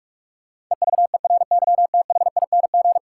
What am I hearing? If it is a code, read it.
Morse code, heard as a keyed tone, about 47 wpm, 700 Hz. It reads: EVERYTHING
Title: Sentences from Top 700 Words - New Word Drill - 45wpm